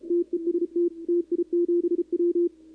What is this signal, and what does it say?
Morse code, heard as a keyed tone, about 34 wpm, 340 Hz, with radio noise. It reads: TEST TI7W
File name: contests / 2019-cw-wpx-kt5j-k5pi- -2330 0-11